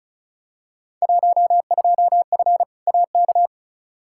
Morse code, heard as a keyed tone, about 34 wpm, 700 Hz. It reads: 12F AK
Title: Field Day Exchanges Encore - 35wpm